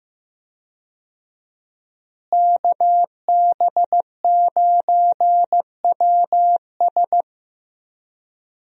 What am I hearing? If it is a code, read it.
Morse code, heard as a keyed tone, about 15 wpm, 700 Hz. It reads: KB9WS
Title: USA Call Signs - 5 Characters - Repeated 2x - 15wpm